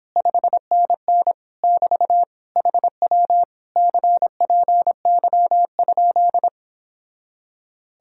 Morse code, heard as a keyed tone, about 29 wpm, 700 Hz. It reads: 5NN <BT> HW CPY?